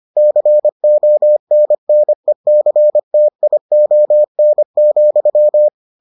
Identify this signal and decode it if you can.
Morse code, heard as a keyed tone, about 25 wpm, 600 Hz. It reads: CONNECTION,